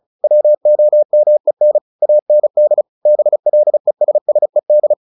Morse code, heard as a keyed tone, about 34 wpm, 600 Hz. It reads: WOMEN AND BLESSED
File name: the old man and the sea / the old man and the sea 1 35wpm